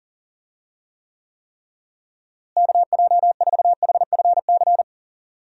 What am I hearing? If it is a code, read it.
Morse code, heard as a keyed tone, about 40 wpm, 700 Hz. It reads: KJ4HFC
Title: USA Call Signs - 6 Characters - Repeated 3x - 40wpm